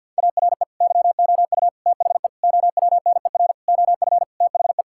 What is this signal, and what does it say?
Morse code, heard as a keyed tone, about 50 wpm, 700 Hz. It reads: ARE YOU THE OWNER OF THE
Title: Sentences from Top 1600 Words 50wpm